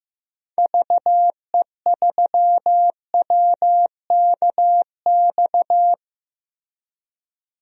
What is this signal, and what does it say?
Morse code, heard as a keyed tone, about 15 wpm, 700 Hz. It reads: VE3WKX
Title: DXCC-100 Call Signs - 6 Characters - Repeated 3x - 15wpm